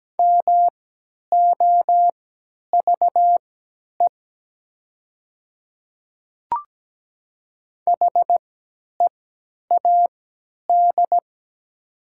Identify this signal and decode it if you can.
Morse code, heard as a keyed tone, about 17 wpm, spaced as for 10 wpm, 700 Hz. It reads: MOVE HEAD
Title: 3x Character Spacing - Top 300 Words - 17wpm